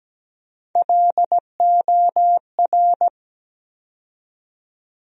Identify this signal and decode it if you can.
Morse code, heard as a keyed tone, about 17 wpm, 700 Hz. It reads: LOR
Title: USA Call Sign Postfixes - 3 Letters 17wpm